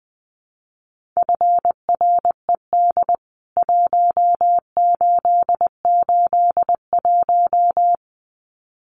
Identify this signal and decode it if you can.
Morse code, heard as a keyed tone, about 20 wpm, 700 Hz. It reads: FRED 1881
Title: CWT Exchange 20wpm